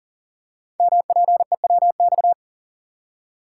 Morse code, heard as a keyed tone, about 41 wpm, 700 Hz. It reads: MPEWX
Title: Code Groups - 5 Letters - 40wpm